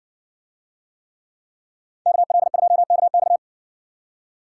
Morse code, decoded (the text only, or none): KL2KX